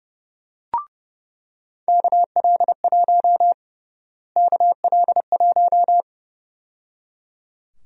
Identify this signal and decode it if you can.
Morse code, heard as a keyed tone, about 29 wpm, 700 Hz. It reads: KL1 KL1
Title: USA Call Sign Prefixes - Repeated 2x - 30wpm